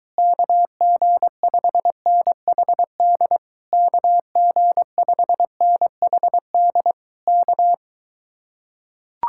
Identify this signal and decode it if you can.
Morse code, heard as a keyed tone, about 24 wpm, 700 Hz. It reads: KG5NHD KG5NHD K